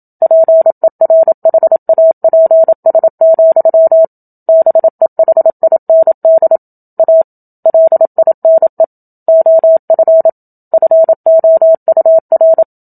Morse code, heard as a keyed tone, about 27 wpm, 650 Hz. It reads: PERHAPS, BEHIND A LINE OF FOUR
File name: Book Two / 01-Chapter 1 - 27 WPM